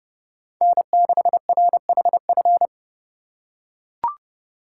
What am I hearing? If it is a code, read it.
Morse code, heard as a keyed tone, about 30 wpm, 700 Hz. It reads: N6RHF